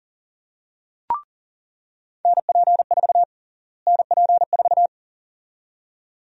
Morse code, heard as a keyed tone, about 40 wpm, 700 Hz. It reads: NP4 NP4
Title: USA Call Sign Prefixes - Repeated 2x - 40wpm